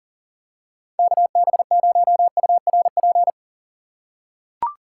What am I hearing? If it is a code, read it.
Morse code, heard as a keyed tone, about 40 wpm, 700 Hz. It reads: KB0URP
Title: USA Call Signs - 6 Characters - 3x Character Spacing - 40wpm